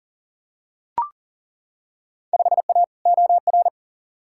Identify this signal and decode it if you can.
Morse code, heard as a keyed tone, about 37 wpm, 700 Hz. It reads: 5A OR
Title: Field Day Exchanges - 40wpm